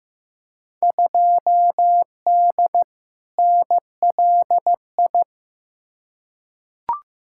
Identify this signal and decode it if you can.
Morse code, heard as a keyed tone, about 15 wpm, 700 Hz. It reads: 2D NLI
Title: Field Day Exchanges - 15wpm